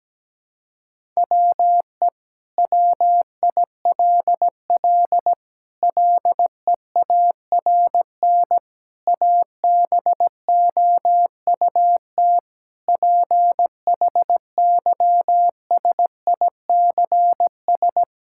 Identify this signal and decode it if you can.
Morse code, heard as a keyed tone, about 17 wpm, 700 Hz. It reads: WE WILL LEARN ABOUT PHYSICS
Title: Sentences from Top 1000 Words - Review Part 5 - 17wpm